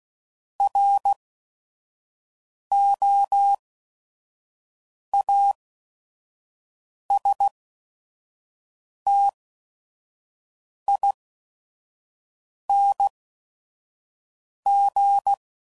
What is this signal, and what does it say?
Morse code, heard as a keyed tone, about 16 wpm, spaced as for 5 wpm, 800 Hz. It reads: ROASTING
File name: Suffixes / Special ING's / VerySlowToFaster ING's